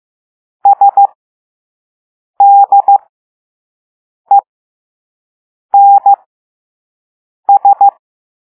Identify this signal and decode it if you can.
Morse code, heard as a keyed tone, about 15 wpm, spaced as for 5 wpm, 800 Hz. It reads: SDENS
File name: Morse / Level3-All-B